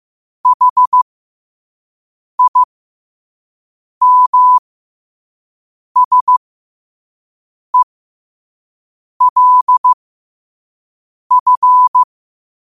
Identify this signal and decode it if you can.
Morse code, heard as a keyed tone, about 15 wpm, spaced as for 5 wpm, 1000 Hz. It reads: HIMSELF
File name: qotdfiles / 5wpm